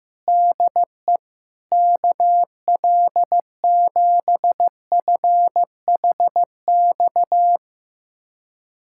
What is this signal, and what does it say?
Morse code, heard as a keyed tone, about 16 wpm, 700 Hz. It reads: DE KL7FHX
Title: USA Callsign DE Callsign - 6 Characters - 15wpm